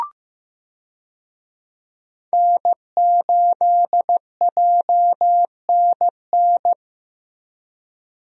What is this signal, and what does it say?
Morse code, heard as a keyed tone, about 15 wpm, 700 Hz. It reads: N8JNN
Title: USA Call Signs - 5 Characters - Repeated 3x - 15wpm